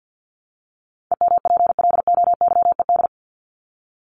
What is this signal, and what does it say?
Morse code, heard as a keyed tone, about 50 wpm, 700 Hz. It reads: EMPLOYED
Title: ED Suffix - Top 100 Words with Base Words 50wpm